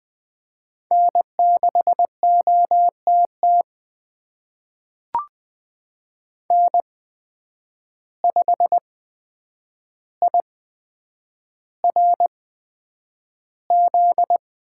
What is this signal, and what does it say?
Morse code, heard as a keyed tone, about 20 wpm, spaced as for 9 wpm, 700 Hz. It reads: N6OTT N5IRZ
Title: USA Call Signs - 5 Characters - 8x Character Spacing - 20wpm